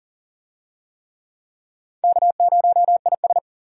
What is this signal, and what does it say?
Morse code, heard as a keyed tone, about 40 wpm, 700 Hz. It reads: K0IS
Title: USA Call Signs - 4 Characters - 40wpm